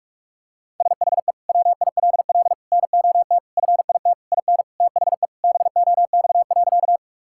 Morse code, heard as a keyed tone, about 46 wpm, 700 Hz. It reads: SHE WILL NOT FIT IN THE BOX.
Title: Sentences from Top 1000 Words - Review Part 5 - 45wpm